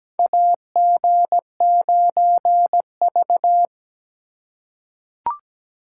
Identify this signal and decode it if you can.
Morse code, heard as a keyed tone, about 17 wpm, 700 Hz. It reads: AG9V
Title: USA Call Signs - 4 Characters - 3x Character Spacing - 17wpm